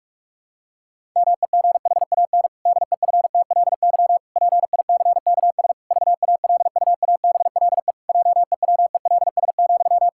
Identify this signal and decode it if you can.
Morse code, heard as a keyed tone, about 41 wpm, 700 Hz. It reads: MEGHAN DEFTLY PICKS VALUABLE JEWELS,
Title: Pangram Challenge v2 - 45wpm